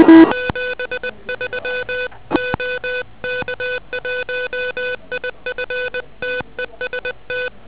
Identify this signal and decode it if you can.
Morse code, heard as a keyed tone, about 20 wpm, 1500 Hz, with noise. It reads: 73 OK1IF TEST
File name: OK QRP 2007 / REC21